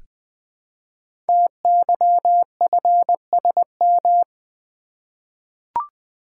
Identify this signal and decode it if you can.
Morse code, heard as a keyed tone, about 20 wpm, 700 Hz. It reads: TYFSM
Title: Code Groups - 5 Letters - 20wpm